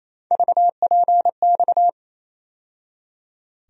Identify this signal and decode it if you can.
Morse code, heard as a keyed tone, about 28 wpm, 700 Hz. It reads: VPX
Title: USA Call Sign Postfixes - 3 Letters 28wpm